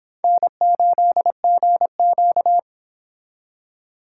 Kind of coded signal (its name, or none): Morse code